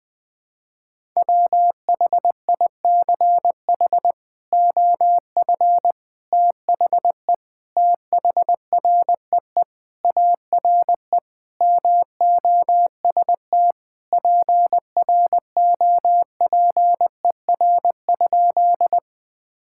Morse code, heard as a keyed tone, about 20 wpm, 700 Hz. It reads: WHICH OF THE THREE ARE MOST PROPER?
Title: Sentences from Top 1000 Words - Review Part 5 - 20wpm